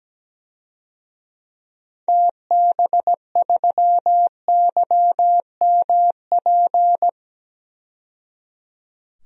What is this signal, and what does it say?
Morse code, heard as a keyed tone, about 17 wpm, 700 Hz. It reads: TB3YMP
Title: DXCC-100 Call Signs - 6 Characters - 17wpm